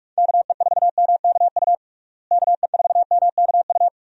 Morse code, heard as a keyed tone, about 47 wpm, 700 Hz. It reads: KE4MKU KE4MKU
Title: USA Call Signs - 6 Characters - Repeated 3x - 45wpm